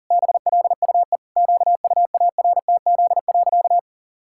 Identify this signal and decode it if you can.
Morse code, heard as a keyed tone, about 40 wpm, 700 Hz. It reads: BLUE QUARTZ.
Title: Pangram Challenge v2 - 40wpm